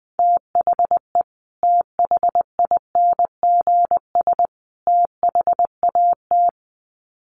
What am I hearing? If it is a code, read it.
Morse code, heard as a keyed tone, about 20 wpm, 700 Hz. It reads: THE THINGS THAT
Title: Sets of 3 Words 20wpm